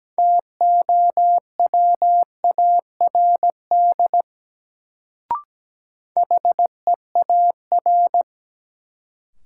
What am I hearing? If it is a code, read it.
Morse code, heard as a keyed tone, about 17 wpm, 700 Hz. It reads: TOWARD HEAR
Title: Sets of 5 Words - New Word Drill - 17wpm